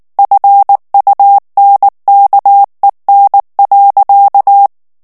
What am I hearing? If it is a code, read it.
Morse code, heard as a keyed tone, about 19 wpm, 800 Hz. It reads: FUNKEN.
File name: morse mp3 / ALLEMAND texte clair 2 vitesse 18 mots minute